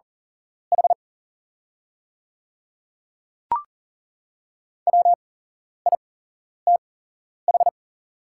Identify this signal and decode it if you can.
Morse code, heard as a keyed tone, about 41 wpm, spaced as for 11 wpm, 690 Hz. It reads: H WITH
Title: 8x Character Spacing - Top 300 Words - 40wpm